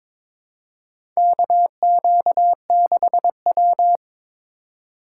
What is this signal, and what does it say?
Morse code, heard as a keyed tone, about 22 wpm, 700 Hz. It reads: KQ6W
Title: USA Call Signs - 4 Characters - 3x Character Spacing - 22wpm